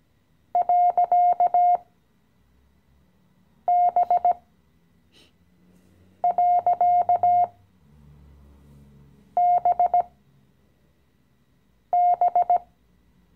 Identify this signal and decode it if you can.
Morse code, heard as a keyed tone, about 17 wpm, spaced as for 4 wpm, 710 Hz. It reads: .B.BB
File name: CW-Online-Kurs / Anfänger / Lektion 015